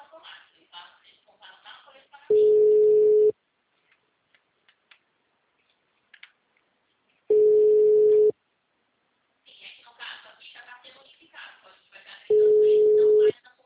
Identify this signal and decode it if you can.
Ringback tone (Europe, 425 Hz), with speech.
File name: out- -2019- - - .107